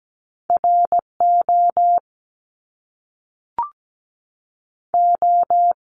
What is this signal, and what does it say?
Morse code, heard as a keyed tone, about 17 wpm, 700 Hz. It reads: RO O
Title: Top Two Letter Combinations in Words 17wpm